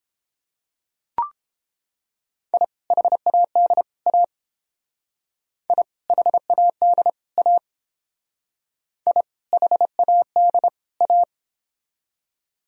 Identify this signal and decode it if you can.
Morse code, heard as a keyed tone, about 30 wpm, 700 Hz. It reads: I HAD A I HAD A I HAD A